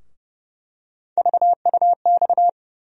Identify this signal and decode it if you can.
Morse code, heard as a keyed tone, about 32 wpm, 700 Hz. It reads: VUX
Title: USA Call Sign Postfixes - 3 Letters 30wpm